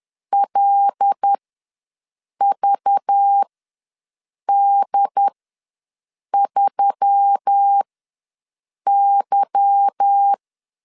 Morse code, heard as a keyed tone, about 10 wpm, spaced as for 6 wpm, 800 Hz. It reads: LVD3Y
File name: LEZIONI / 8 YJ45 / 8 Y ws 10wpm 3952